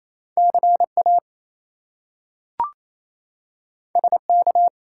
Morse code, heard as a keyed tone, about 28 wpm, 700 Hz. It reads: CA SK
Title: US States and Canadian Provinces 28wpm